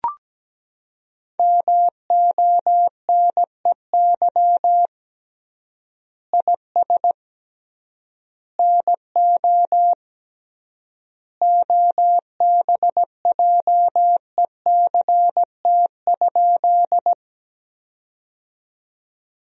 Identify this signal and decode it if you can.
Morse code, heard as a keyed tone, about 17 wpm, 700 Hz. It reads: MONEY IS NO OBJECT?
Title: 3x Word-Spacing - Sentences from Top 500 Words - 17wpm